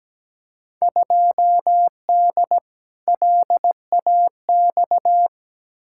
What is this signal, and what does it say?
Morse code, heard as a keyed tone, about 17 wpm, 700 Hz. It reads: 2D LAX
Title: Field Day Exchanges - 17wpm